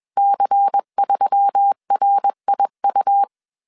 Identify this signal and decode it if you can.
Morse code, heard as a keyed tone, about 21 wpm, 800 Hz. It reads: C3RIU
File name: CW / DET / 1 6 1500C S B 20wpm 2048